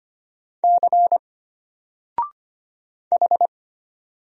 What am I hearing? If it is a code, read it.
Morse code, heard as a keyed tone, about 25 wpm, 700 Hz. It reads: C H
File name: Single-Letter-Number-25wpm